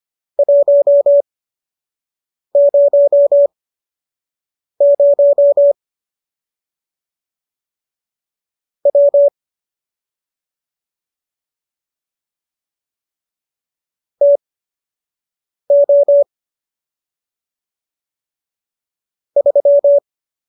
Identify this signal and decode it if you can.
Morse code, heard as a keyed tone, about 25 wpm, spaced as for 6 wpm, 570 Hz. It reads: 100 W TO 3